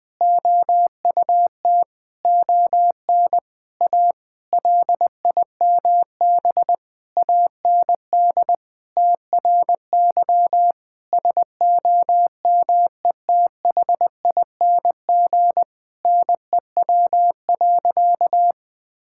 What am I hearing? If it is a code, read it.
Morse code, heard as a keyed tone, about 20 wpm, 700 Hz. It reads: OUT ON A LIMB AND TRY SOMETHING NEW.
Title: Sentences with 8-Word Phrases - Spoken Beginning - 20wpm